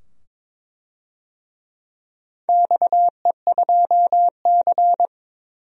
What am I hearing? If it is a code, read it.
Morse code, heard as a keyed tone, about 22 wpm, 700 Hz. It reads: XE2C
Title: DXCC-100 Call Signs - 4 Characters - 22wpm